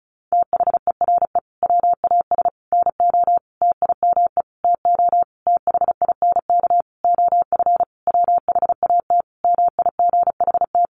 Morse code, heard as a keyed tone, about 34 wpm, 700 Hz. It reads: THERE WAS NO TIME TO THINK OF WHAT MIGHT
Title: Sentences from Top 400 Words 35wpm